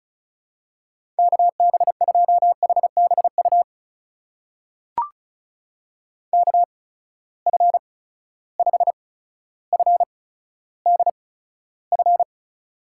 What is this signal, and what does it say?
Morse code, heard as a keyed tone, about 36 wpm, spaced as for 15 wpm, 700 Hz. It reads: KB2HBU KF5FDF
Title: USA Call Signs - 6 Characters - 8x Character Spacing - 35wpm